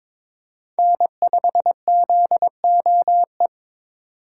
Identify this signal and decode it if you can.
Morse code, heard as a keyed tone, about 28 wpm, 700 Hz. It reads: N5ZOE
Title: Speed Racing - USA Call Signs 22wpm